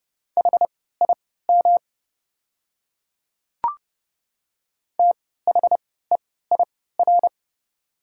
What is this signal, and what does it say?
Morse code, heard as a keyed tone, about 30 wpm, spaced as for 17 wpm, 700 Hz. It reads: HIM THEIR
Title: 3x Character Spacing - Top 100 Words - 30wpm